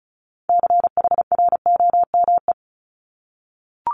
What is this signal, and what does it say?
Morse code, heard as a keyed tone, about 36 wpm, 700 Hz. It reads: CHROME
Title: Multi-letter Phonemes Repeated 3x 35wpm